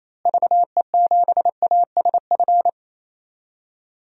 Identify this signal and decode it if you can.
Morse code, heard as a keyed tone, about 28 wpm, 700 Hz. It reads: VE7ASF
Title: DXCC-100 Call Signs - 6 Characters - Repeated 3x - 28wpm